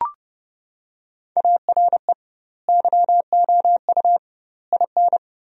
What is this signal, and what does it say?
Morse code, heard as a keyed tone, about 30 wpm, 700 Hz. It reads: ARE YOU IN
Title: 2x Word-Spacing - Sentences from Top 400 Words - 30wpm